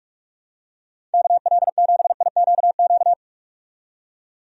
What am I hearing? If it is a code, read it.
Morse code, heard as a keyed tone, about 44 wpm, 700 Hz. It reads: KL7IQQ
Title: DXCC-100 Call Signs - 6 Characters - 45wpm